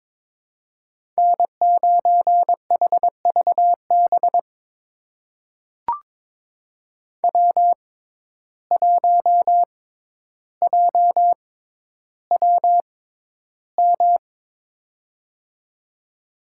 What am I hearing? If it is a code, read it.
Morse code, heard as a keyed tone, about 21 wpm, spaced as for 11 wpm, 700 Hz. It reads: N9HVB W1JWM
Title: USA Call Signs - 6x Character Spacing - 22wpm